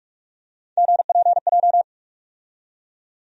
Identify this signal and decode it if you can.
Morse code, heard as a keyed tone, about 46 wpm, 700 Hz. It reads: GPJ
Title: USA Call Sign Postfixes - 3 Letters 45wpm